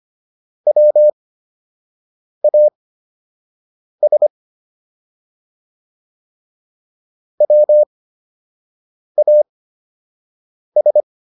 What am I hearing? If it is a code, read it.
Morse code, heard as a keyed tone, about 25 wpm, spaced as for 6 wpm, 600 Hz. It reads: WAS WAS